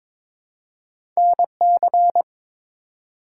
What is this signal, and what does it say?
Morse code, heard as a keyed tone, about 21 wpm, 700 Hz. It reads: NC
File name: ARRL Sections v2 - 22wpm